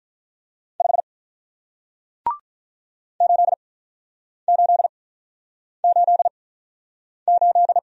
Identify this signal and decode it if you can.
Morse code, heard as a keyed tone, about 44 wpm, spaced as for 9 wpm, 700 Hz. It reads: 5 8888